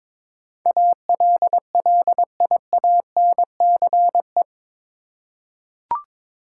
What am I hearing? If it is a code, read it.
Morse code, heard as a keyed tone, about 22 wpm, 700 Hz. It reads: ALLIANCE
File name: practice / 22wpm / Eight Letter Words - 22wpm